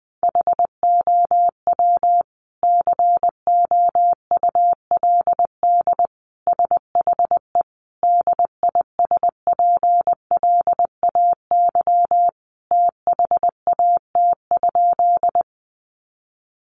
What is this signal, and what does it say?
Morse code, heard as a keyed tone, about 20 wpm, 700 Hz. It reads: HOW COULD SHE DISPLAY THAT?
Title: Sentences from Top 1200 Words 20wpm